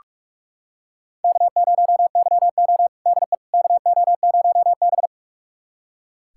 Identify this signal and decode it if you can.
Morse code, heard as a keyed tone, about 41 wpm, 700 Hz. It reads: K0YO DE KO0B
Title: USA Callsign DE Callsign - 4 Characters - 45wpm